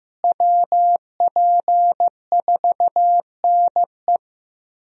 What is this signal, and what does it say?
Morse code, heard as a keyed tone, about 15 wpm, 700 Hz. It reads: WP4NE
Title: USA Call Signs - 5 Characters - Repeated 2x - 15wpm